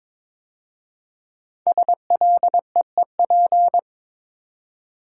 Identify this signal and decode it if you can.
Morse code, heard as a keyed tone, about 22 wpm, 700 Hz. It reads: SLEEP
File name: 5 Letter Words - First 3 Letters Given - 22wpm